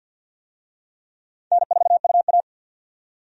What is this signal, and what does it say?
Morse code, heard as a keyed tone, about 48 wpm, 700 Hz. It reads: N4UA
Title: USA Call Signs - 4 Characters - 50wpm